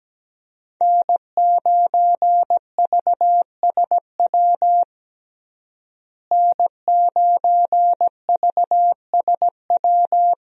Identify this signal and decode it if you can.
Morse code, heard as a keyed tone, about 17 wpm, 700 Hz. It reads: N9VSW N9VSW
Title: USA Call Signs - 5 Characters - Repeated 2x - 17wpm